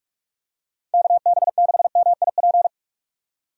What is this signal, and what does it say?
Morse code, heard as a keyed tone, about 43 wpm, 700 Hz. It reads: KB6MIP
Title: USA Call Signs - 6 Characters - 45wpm